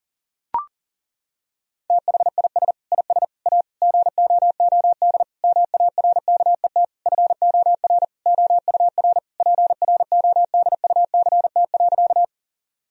Morse code, heard as a keyed tone, about 40 wpm, 700 Hz. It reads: THIS IS A GOOD MARKET FOR OUR PRODUCT.
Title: Sentences from Top 1000 Words - Review Part 3 - 40wpm